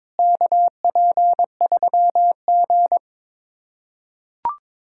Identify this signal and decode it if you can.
Morse code, heard as a keyed tone, about 22 wpm, 700 Hz. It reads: KP3G
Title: USA Callsign DE Callsign - 4 Characters - 22wpm